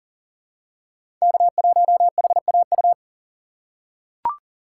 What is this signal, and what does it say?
Morse code, heard as a keyed tone, about 40 wpm, 700 Hz. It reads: K1HAU
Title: USA Call Signs - 5 Characters - 2x Character Spacing - 40wpm